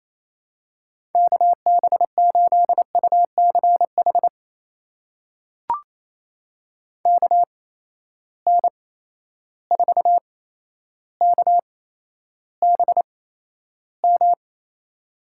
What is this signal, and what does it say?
Morse code, heard as a keyed tone, about 28 wpm, spaced as for 12 wpm, 700 Hz. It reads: KB8UCH KN4KBM
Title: USA Call Signs - 6 Characters - 8x Character Spacing - 28wpm